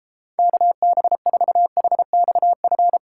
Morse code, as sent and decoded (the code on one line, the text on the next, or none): -.- -... ....- .... -..- ..-.
KB4HXF